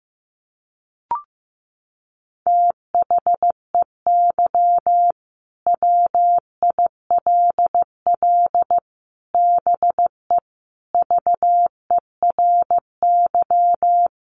Morse code, heard as a keyed tone, about 15 wpm, 700 Hz. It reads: THEY WILL BE VERY
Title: Sentences from Top 1100 Words 15wpm